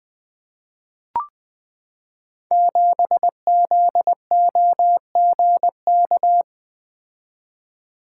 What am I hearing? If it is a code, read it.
Morse code, heard as a keyed tone, about 20 wpm, 700 Hz. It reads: 7ZOGK
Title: Code Groups - 5 Letter:Numbers - 20wpm